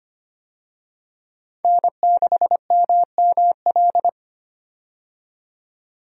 Morse code, heard as a keyed tone, about 25 wpm, 700 Hz. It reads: N6MML
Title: USA Call Signs - 5 Characters - 25wpm